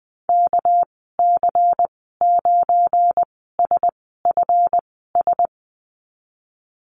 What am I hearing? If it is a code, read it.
Morse code, heard as a keyed tone, about 20 wpm, spaced as for 15 wpm, 700 Hz. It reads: KC9SFS
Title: 200 Lesson 32g - Letter k - Call Signs 20wpm